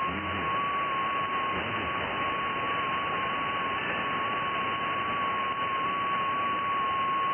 FT8.
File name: IC-7100